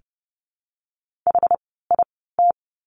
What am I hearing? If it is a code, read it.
Morse code, heard as a keyed tone, about 30 wpm, spaced as for 17 wpm, 700 Hz. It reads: HIT
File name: Lesson 17f - Letter u - Words 30wpm